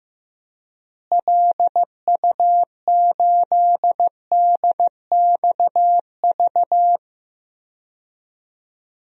Morse code, heard as a keyed tone, about 15 wpm, 700 Hz. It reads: LU8DXV
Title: DXCC-100 Call Signs - 6 Characters - Repeated 2x - 15wpm